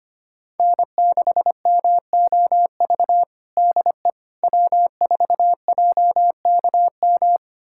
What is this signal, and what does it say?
Morse code, heard as a keyed tone, about 25 wpm, 700 Hz. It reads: N6MOV DE W4JKM
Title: USA Callsign DE Callsign - 5 Characters - 25wpm